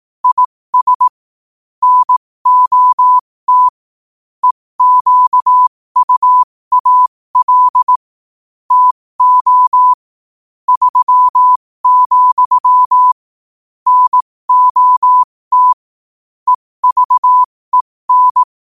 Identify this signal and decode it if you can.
Morse code, heard as a keyed tone, about 18 wpm, spaced as for 15 wpm, 1000 Hz. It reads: IS NOT EQUAL TO 3, NOT EVEN
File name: qotdfiles / 15wpm